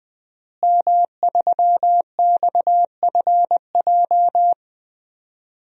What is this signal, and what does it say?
Morse code, heard as a keyed tone, about 20 wpm, 700 Hz. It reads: M3XFJ
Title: DXCC-100 Call Signs - 5 Characters - Repeated 2x - 20wpm